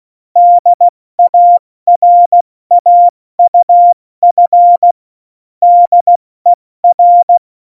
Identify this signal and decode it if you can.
Morse code, heard as a keyed tone, about 16 wpm, 700 Hz. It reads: DARAUF DER
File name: Morsekurs / Witze / Witz 11. September 2018, 70 F80